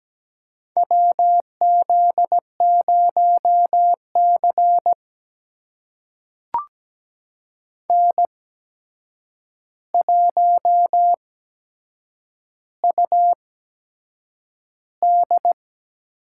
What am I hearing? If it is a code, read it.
Morse code, heard as a keyed tone, about 17 wpm, spaced as for 7 wpm, 700 Hz. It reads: WZ0C N1UD
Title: USA Call Signs - 8x Character Spacing - 17wpm